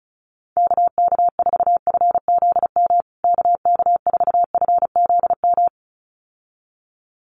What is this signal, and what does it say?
Morse code, heard as a keyed tone, about 34 wpm, 700 Hz. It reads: KK4FZM KK4FZM
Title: USA Call Signs Repeated 2x 35wpm